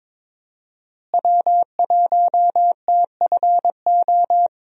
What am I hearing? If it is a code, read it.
Morse code, heard as a keyed tone, about 22 wpm, 700 Hz. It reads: W1TFO
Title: USA Call Signs - 5 Characters - Repeated 2x - 22wpm